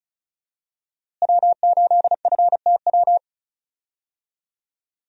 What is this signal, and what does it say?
Morse code, heard as a keyed tone, about 34 wpm, 700 Hz. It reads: W8FTW
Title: USA Call Signs - 5 Characters - Repeated 2x - 35wpm